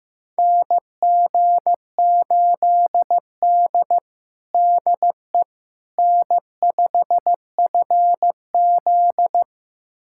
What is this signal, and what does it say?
Morse code, heard as a keyed tone, about 16 wpm, 700 Hz. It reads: NG8D DE N5FZ
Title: USA Callsign DE Callsign - 4 Characters - 15wpm